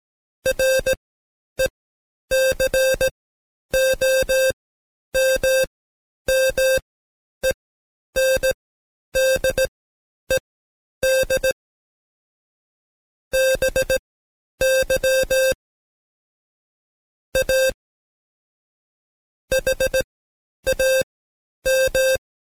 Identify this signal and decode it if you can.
Morse code, heard as a keyed tone, about 17 wpm, spaced as for 10 wpm, 530 Hz. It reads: RECOMMENDED BY A HAM
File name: Code Quick / AudioDownld2 / Seg131-Stepper4 9WPM